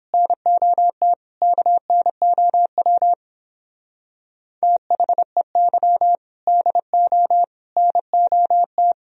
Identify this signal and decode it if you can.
Morse code, heard as a keyed tone, about 29 wpm, 700 Hz. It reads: NOT KNOW THEY DO NOT